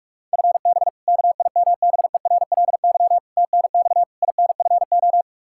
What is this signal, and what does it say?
Morse code, heard as a keyed tone, about 46 wpm, 700 Hz. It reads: FB KIMBERLY TNX INFO
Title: Practice QSO - Exchange 3 - More Info - 45wpm